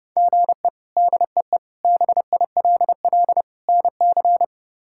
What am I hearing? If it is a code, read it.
Morse code, heard as a keyed tone, about 30 wpm, 700 Hz. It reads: GE DEE BILL NC
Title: K1USN SST QSO Practice - 30wpm